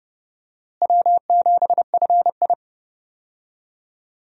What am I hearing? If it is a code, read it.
Morse code, heard as a keyed tone, about 32 wpm, 700 Hz. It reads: W7FI
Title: USA Call Signs - 4 Characters - Repeated 3x - 30wpm